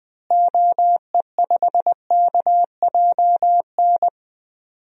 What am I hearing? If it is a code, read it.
Morse code, heard as a keyed tone, about 21 wpm, 700 Hz. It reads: OE5KJN
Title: DXCC-100 Call Signs - 6 Characters - Repeated 3x - 20wpm